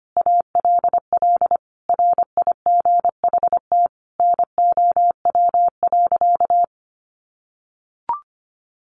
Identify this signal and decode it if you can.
Morse code, heard as a keyed tone, about 25 wpm, 700 Hz. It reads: ALL RIGHT NOW.
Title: Sentences from Top 700 Words 25wpm